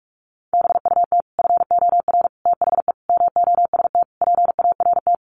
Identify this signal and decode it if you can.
Morse code, heard as a keyed tone, about 44 wpm, 700 Hz. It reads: BUT FOR THE MOST PART
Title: Sets of 5 Words Encore 45wpm